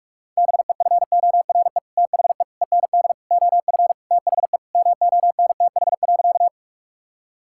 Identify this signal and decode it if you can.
Morse code, heard as a keyed tone, about 46 wpm, 700 Hz. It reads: BEFORE THE END OF THE MONTH.
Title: Sentences from Top 1000 Words - Review Part 5 - 45wpm